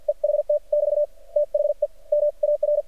Morse code, heard as a keyed tone, about 50 wpm, spaced as for 43 wpm, 590 Hz. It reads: EHT <BT> THE M AR
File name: archived / CRSnet